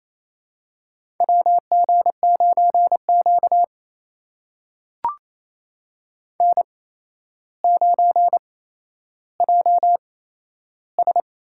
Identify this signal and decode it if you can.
Morse code, heard as a keyed tone, about 28 wpm, spaced as for 12 wpm, 700 Hz. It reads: WG9Q N9JS
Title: USA Call Signs - 4 Characters - 8x Character Spacing - 28wpm